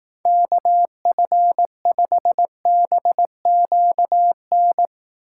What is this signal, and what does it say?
Morse code, heard as a keyed tone, about 18 wpm, 700 Hz. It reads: KF5BQN
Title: Speed Racing - USA 2x3 Call Signs 15wpm